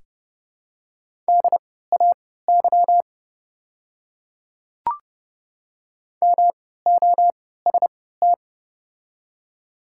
Morse code, heard as a keyed tone, about 30 wpm, spaced as for 17 wpm, 700 Hz. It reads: DAY MOST
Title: 3x Character Spacing - Top 100 Words - 30wpm